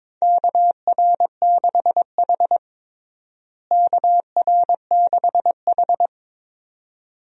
Morse code, heard as a keyed tone, about 22 wpm, 700 Hz. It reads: KR6H KR6H
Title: USA Call Signs - 4 Characters - Repeated 2x - 22wpm